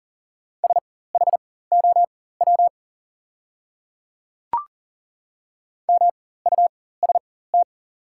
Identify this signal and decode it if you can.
Morse code, heard as a keyed tone, about 40 wpm, spaced as for 19 wpm, 700 Hz. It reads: SHOW MUST
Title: 4x Character Spacing - Top 300 Words - 40wpm